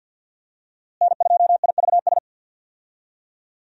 Morse code, heard as a keyed tone, about 46 wpm, 700 Hz. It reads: N2IVS
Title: USA Call Signs - 5 Characters - Repeated 2x - 50wpm